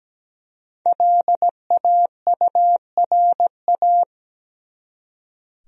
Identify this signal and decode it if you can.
Morse code, heard as a keyed tone, about 17 wpm, 700 Hz. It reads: LAURA
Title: USA Common Names - 17wpm